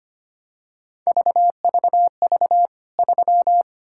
Morse code, heard as a keyed tone, about 25 wpm, 700 Hz. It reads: VVV 3
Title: K1USN SST QSO Practice 25wpm